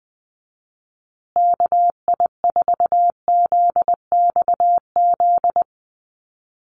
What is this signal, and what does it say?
Morse code, heard as a keyed tone, about 20 wpm, 700 Hz. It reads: KI4ZXZ
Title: USA Call Signs Repeated 2x 20wpm